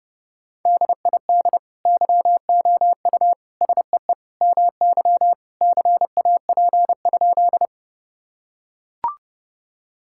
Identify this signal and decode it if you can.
Morse code, heard as a keyed tone, about 32 wpm, 700 Hz. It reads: DID YOU SEE MY CAP?
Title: Sentences from Top 1600 Words 30wpm